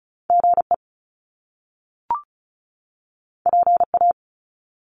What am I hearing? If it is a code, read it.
Morse code, heard as a keyed tone, about 36 wpm, 700 Hz. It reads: GE PA
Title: Top Two Letter Combinations in Words 35wpm